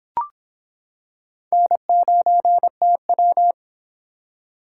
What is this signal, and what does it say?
Morse code, heard as a keyed tone, about 25 wpm, 700 Hz. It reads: N9TW